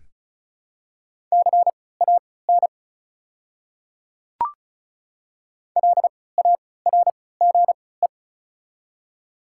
Morse code, heard as a keyed tone, about 36 wpm, spaced as for 20 wpm, 700 Hz. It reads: CAN LARGE
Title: 3x Character Spacing - Top 200 Words - 35wpm